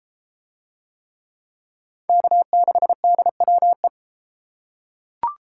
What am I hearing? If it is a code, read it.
Morse code, heard as a keyed tone, about 31 wpm, 700 Hz. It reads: K6DWE